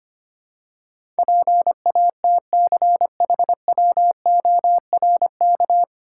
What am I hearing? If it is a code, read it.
Morse code, heard as a keyed tone, about 25 wpm, 700 Hz. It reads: PATCHWORK
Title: Pangram Challenge v2 - 25wpm